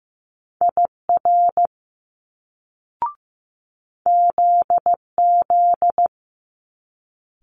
Morse code, heard as a keyed tone, about 15 wpm, 700 Hz. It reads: IR ZZ
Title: Multi-letter Phonemes Quick Fire 15wpm